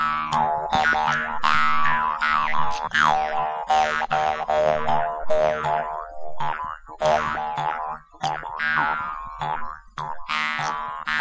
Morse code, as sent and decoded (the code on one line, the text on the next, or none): none